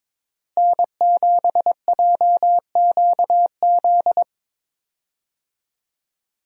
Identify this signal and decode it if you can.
Morse code, heard as a keyed tone, about 22 wpm, 700 Hz. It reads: N7JQZ
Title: USA Call Signs - 5 Characters - Repeated 3x - 22wpm